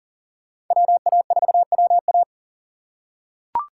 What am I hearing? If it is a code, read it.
Morse code, heard as a keyed tone, about 40 wpm, 700 Hz. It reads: WA4WA
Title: USA Call Signs - 5 Characters - 4x Character Spacing - 40wpm